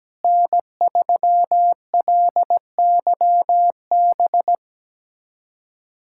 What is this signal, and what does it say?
Morse code, heard as a keyed tone, about 17 wpm, 700 Hz. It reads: N3LYB
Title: USA Callsign DE Callsign 17wpm